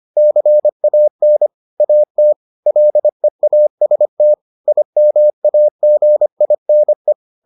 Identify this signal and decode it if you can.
Morse code, heard as a keyed tone, about 25 wpm, 600 Hz. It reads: CAN AT LEAST IMAGINE